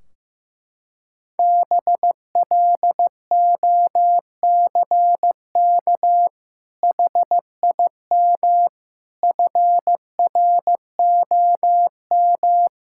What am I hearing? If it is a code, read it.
Morse code, heard as a keyed tone, about 15 wpm, 700 Hz. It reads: BLOCK HIM FROM
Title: Sentences from Top 1600 Words 15wpm